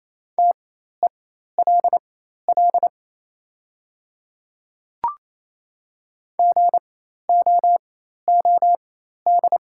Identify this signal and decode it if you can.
Morse code, heard as a keyed tone, about 28 wpm, spaced as for 13 wpm, 690 Hz. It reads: TELL GOOD
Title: 4x Character Spacing - Top 200 Words - 28wpm